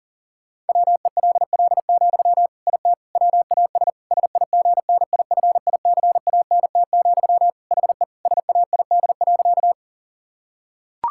Morse code, heard as a keyed tone, about 40 wpm, 700 Hz. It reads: WELL, IT WAS SIGNIFICANT, HE SAID.
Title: Sentences from Top 1000 Words - Review Part 6 - 40wpm